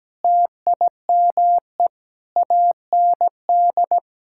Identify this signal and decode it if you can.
Morse code, heard as a keyed tone, about 17 wpm, 700 Hz. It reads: TIME AND
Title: Sentences from Top 1000 Words - Review Part 5 - 17wpm